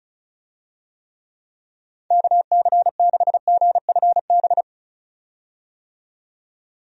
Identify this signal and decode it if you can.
Morse code, heard as a keyed tone, about 36 wpm, 700 Hz. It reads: KC6GFB
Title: USA Call Signs - 6 Characters - 35wpm